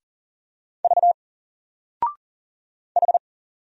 Morse code, heard as a keyed tone, about 40 wpm, 700 Hz. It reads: V H
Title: Single Letter-Number 40wpm